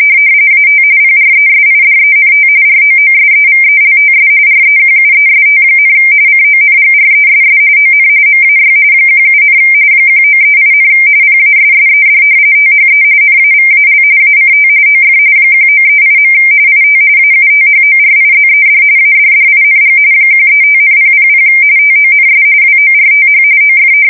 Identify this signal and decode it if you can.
RTTY, about 45.45 baud. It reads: AT PRESIDENT ROOSEVELTS REQUEST, CONGRESS ON MAY 17, 1938, PASSED THE NAVAL ACT OF 1938. IT AUTHORIZED ADDING 135,000 TONS WORTH OF
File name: fsk170 / battleshipnj